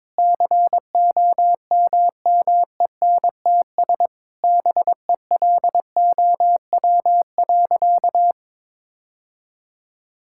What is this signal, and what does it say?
Morse code, heard as a keyed tone, about 22 wpm, 700 Hz. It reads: COMMENTS BELOW.
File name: Sentences with 7-Word Phrases - 22wpm